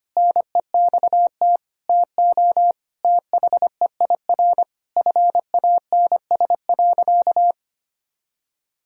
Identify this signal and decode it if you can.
Morse code, heard as a keyed tone, about 25 wpm, 700 Hz. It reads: NEXT TO THEIR FANS.
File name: practice / 25wpm / All-Age Appropriate Jokes - 25wpm